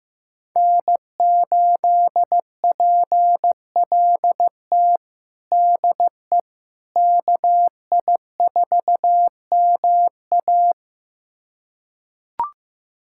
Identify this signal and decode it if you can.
Morse code, heard as a keyed tone, about 16 wpm, 700 Hz. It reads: N8PLT DE KI4MA
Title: USA Callsign DE Callsign - 5 Characters - 15wpm